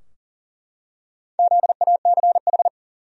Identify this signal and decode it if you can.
Morse code, heard as a keyed tone, about 40 wpm, 700 Hz. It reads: ZACH
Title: USA Common Names - Max 4 Letters - 40wpm